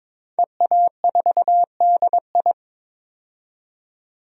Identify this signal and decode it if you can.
Morse code, heard as a keyed tone, about 22 wpm, 700 Hz. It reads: EA4DI
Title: DXCC-100 Call Signs - 5 Characters - 22wpm